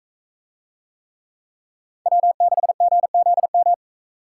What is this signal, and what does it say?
Morse code, heard as a keyed tone, about 40 wpm, 700 Hz. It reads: W6GZM